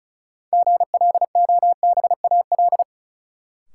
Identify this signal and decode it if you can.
Morse code, heard as a keyed tone, about 36 wpm, 700 Hz. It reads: GLOBAL
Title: Sentences from Top 1200 Words - New Word Drill Alt - 35wpm